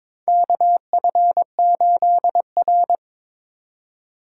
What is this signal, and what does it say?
Morse code, heard as a keyed tone, about 21 wpm, 700 Hz. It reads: KF8R
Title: USA Call Signs - 3x Character Spacing - 22wpm